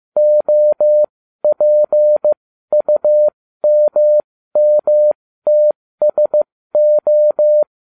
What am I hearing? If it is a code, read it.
Morse code, heard as a keyed tone, about 15 wpm, spaced as for 13 wpm, 600 Hz. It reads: OPUMMTSO